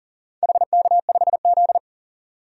Morse code, heard as a keyed tone, about 40 wpm, 700 Hz. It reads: HK5Z
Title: DXCC-100 Call Signs - 4 Characters - 40wpm